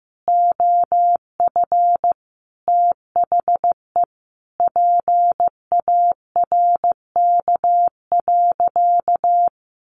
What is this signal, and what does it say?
Morse code, heard as a keyed tone, about 15 wpm, 700 Hz. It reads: OF THE PARK.
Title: Sentences from Top 1300 Words 15wpm